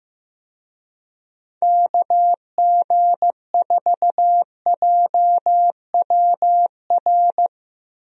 Morse code, heard as a keyed tone, about 15 wpm, 700 Hz. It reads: KG4JWR
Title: USA Call Signs - 6 Characters - Repeated 2x - 15wpm